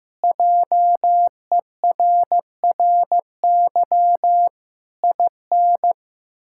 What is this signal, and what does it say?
Morse code, heard as a keyed tone, about 15 wpm, 700 Hz. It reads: JERRY IN
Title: K1USN SST QSO Practice - 15wpm